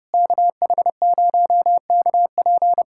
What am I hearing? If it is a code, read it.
Morse code, heard as a keyed tone, about 30 wpm, 700 Hz. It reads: KH0KP
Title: USA Callsign DE Callsign 30wpm